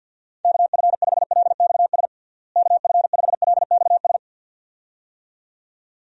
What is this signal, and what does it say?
Morse code, heard as a keyed tone, about 49 wpm, 700 Hz. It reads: KF5LXS KF5LXS
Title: USA Call Signs - 6 Characters - Repeated 2x - 50wpm